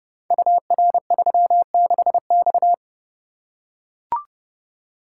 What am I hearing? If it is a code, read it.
Morse code, heard as a keyed tone, about 30 wpm, 700 Hz. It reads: UR36X
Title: Code Groups - 5 Letter:Numbers - 30wpm